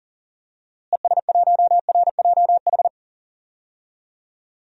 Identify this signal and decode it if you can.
Morse code, heard as a keyed tone, about 40 wpm, 700 Hz. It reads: ES1RJH
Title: DXCC-100 Call Signs - 6 Characters - Repeated 2x - 40wpm